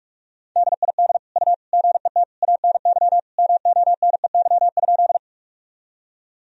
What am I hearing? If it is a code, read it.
Morse code, heard as a keyed tone, about 46 wpm, 700 Hz. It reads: DID U GET ANY MONEY?
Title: Sentences from Top 400 Words and Abbreviations 45wpm